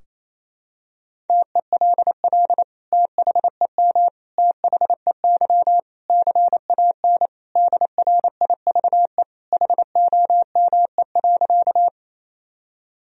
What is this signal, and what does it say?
Morse code, heard as a keyed tone, about 27 wpm, 700 Hz. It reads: TELL THEM THEY CAN DRIVE HOME.
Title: Sentences from Top 1000 Words - Review Part 2 - 28wpm